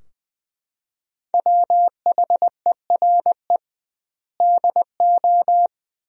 Morse code, heard as a keyed tone, about 21 wpm, 700 Hz. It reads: WHERE DO
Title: 2x Word-Spacing - Sentences from Top 200 Words - 20wpm